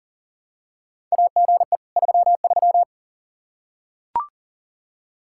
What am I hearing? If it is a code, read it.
Morse code, heard as a keyed tone, about 40 wpm, 700 Hz. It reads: AGE 33
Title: Practice QSO - Exchange 3 - More Info - 40wpm